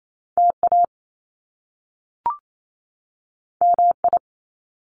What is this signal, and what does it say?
Morse code, heard as a keyed tone, about 28 wpm, 700 Hz. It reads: TA MI
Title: Top Two Letter Combinations in Words 28wpm